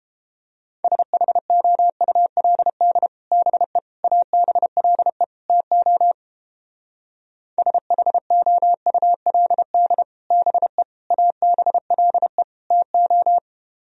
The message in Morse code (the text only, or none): SHOULD BE ABLE TO SHOULD BE ABLE TO